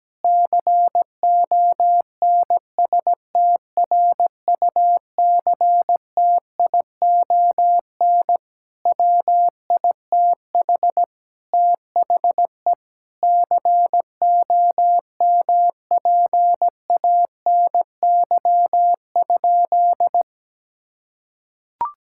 Morse code, heard as a keyed tone, about 17 wpm, 700 Hz. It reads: CONSTRUCTION WITH THE COMPANY?
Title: Sentences from Top 1600 Words 17wpm